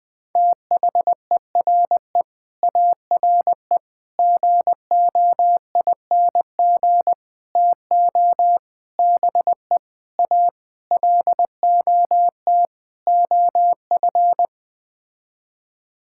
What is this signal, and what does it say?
Morse code, heard as a keyed tone, about 21 wpm, 700 Hz. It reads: THERE ARE GOING TO BE A LOT OF
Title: Sets of 8 Words - 20wpm